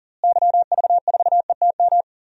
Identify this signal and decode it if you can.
Morse code, heard as a keyed tone, about 40 wpm, 700 Hz. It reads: YV4ETM
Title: DXCC-100 Call Signs - 6 Characters - Repeated 3x - 40wpm